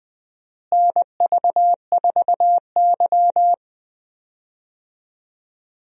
Morse code, heard as a keyed tone, about 20 wpm, 700 Hz. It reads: NV4Y
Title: USA Call Signs - 4 Characters - Repeated 3x - 20wpm